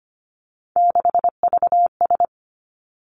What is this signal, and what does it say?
Morse code, heard as a keyed tone, about 25 wpm, 700 Hz. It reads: 6VS
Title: Three Challenging Letter-Number Combinations 25wpm